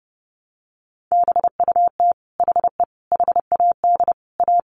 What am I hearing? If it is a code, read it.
Morse code, heard as a keyed tone, about 30 wpm, 700 Hz. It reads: BUT HE HAD A
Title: Sentences from Top 1200 Words 30wpm